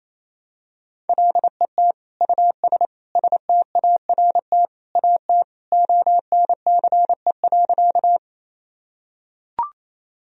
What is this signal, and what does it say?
Morse code, heard as a keyed tone, about 28 wpm, 700 Hz. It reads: LET US START AT ONCE.
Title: Sentences from Top 1000 Words - Review Part 1 - 28wpm